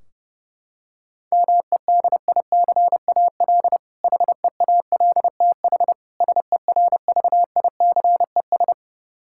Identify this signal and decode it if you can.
Morse code, heard as a keyed tone, about 32 wpm, 700 Hz. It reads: MEDICAL HEALTH SERVICES
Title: K1USN SST QSO Practice 30wpm